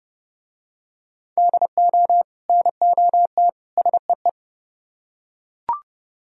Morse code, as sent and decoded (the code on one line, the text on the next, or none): -.. --- / -. --- - / ... .
DO NOT SEE